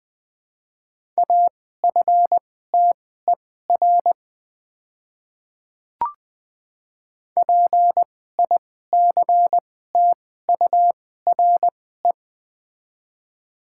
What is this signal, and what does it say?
Morse code, heard as a keyed tone, about 20 wpm, spaced as for 14 wpm, 700 Hz. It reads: AFTER PICTURE
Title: 2x Character Spacing - Top 200 Words - 20wpm